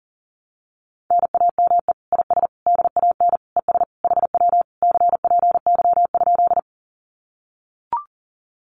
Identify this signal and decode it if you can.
Morse code, heard as a keyed tone, about 38 wpm, 700 Hz. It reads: NAME IS DAN ES HW CPY?
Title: Common QSO Phrases Encore 40wpm